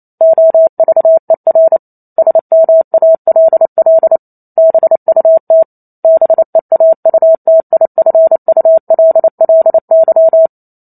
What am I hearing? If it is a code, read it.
Morse code, heard as a keyed tone, about 28 wpm, 650 Hz. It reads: OVER SMALL BUT BEAUTIFULLY
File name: Chapter 2 - 28 WPM